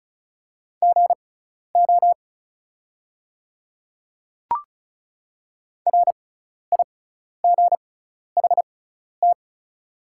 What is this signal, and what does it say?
Morse code, heard as a keyed tone, about 36 wpm, spaced as for 12 wpm, 700 Hz. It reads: GO RIGHT
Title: 6x Character Spacing - Top 200 Words - 35wpm